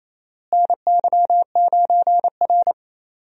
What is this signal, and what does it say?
Morse code, heard as a keyed tone, about 28 wpm, 700 Hz. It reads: NY9R
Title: USA Call Signs - 4 Characters - Repeated 3x - 28wpm